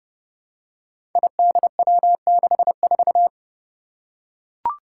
Morse code, heard as a keyed tone, about 30 wpm, 700 Hz. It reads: IDW64
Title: Five Challenging Letter-Number Combinations 30wpm